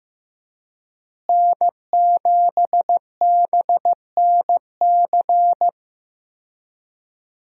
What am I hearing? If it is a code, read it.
Morse code, heard as a keyed tone, about 15 wpm, 700 Hz. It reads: N7BNC
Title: USA Call Signs - 5 Characters - Repeated 2x - 15wpm